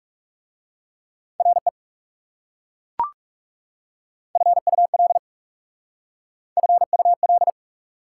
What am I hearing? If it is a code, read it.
Morse code, heard as a keyed tone, about 38 wpm, 690 Hz. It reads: RE FUL FUL